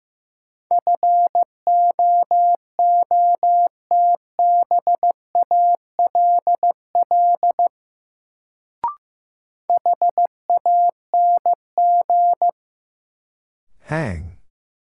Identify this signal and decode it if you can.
Morse code, heard as a keyed tone, about 15 wpm, 700 Hz. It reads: FOOTBALL HANG
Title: Sentences from Top 1600 Words - New Word Drill Alt - 15wpm